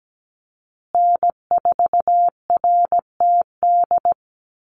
Morse code, heard as a keyed tone, about 17 wpm, 700 Hz. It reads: N4RTD
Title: USA Call Signs Repeated 3x 17wpm